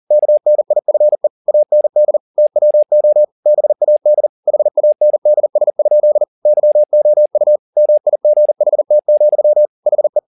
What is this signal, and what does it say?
Morse code, heard as a keyed tone, about 40 wpm, 600 Hz. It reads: KNIFE AND TWO BAD HANDS? YOU MIGHT, HE
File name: the old man and the sea 2 40wpm